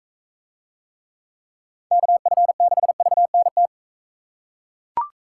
Morse code, heard as a keyed tone, about 41 wpm, 700 Hz. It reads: KF6VNT